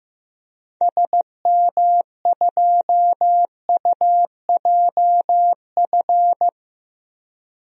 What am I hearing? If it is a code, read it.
Morse code, heard as a keyed tone, about 15 wpm, 700 Hz. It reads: SM2UJF
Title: DXCC-100 Call Signs - 6 Characters - Repeated 2x - 15wpm